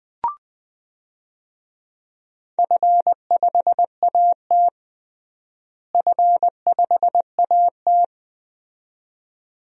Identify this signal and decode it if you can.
Morse code, heard as a keyed tone, about 20 wpm, 700 Hz. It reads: F5AT F5AT
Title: DXCC-100 Call Signs - 4 Characters - Repeated 2x - 20wpm